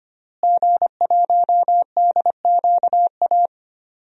Morse code, heard as a keyed tone, about 25 wpm, 700 Hz. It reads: G1DQA
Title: DXCC-100 Call Signs - 5 Characters - Repeated 2x - 25wpm